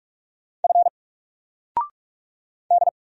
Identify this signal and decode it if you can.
Morse code, heard as a keyed tone, about 44 wpm, 700 Hz. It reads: F D
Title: Single Letter-Number Alt 45wpm